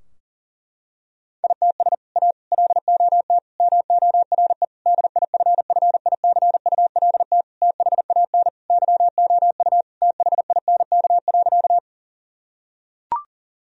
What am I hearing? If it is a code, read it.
Morse code, heard as a keyed tone, about 40 wpm, 700 Hz. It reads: ITS A LOT MORE DIFFICULT THAN YOU THINK.
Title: Sentences from Top 1000 Words - Review Part 3 - 40wpm